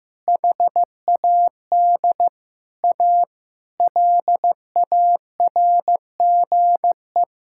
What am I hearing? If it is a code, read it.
Morse code, heard as a keyed tone, about 15 wpm, 700 Hz. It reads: HAD A LARGE
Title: Sentences from Top 1000 Words - Review Part 6 - 15wpm